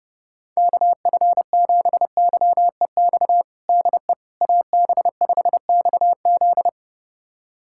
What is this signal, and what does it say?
Morse code, heard as a keyed tone, about 32 wpm, 700 Hz. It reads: KF7YEX DE AB5XZ
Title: USA Callsign DE Callsign 30wpm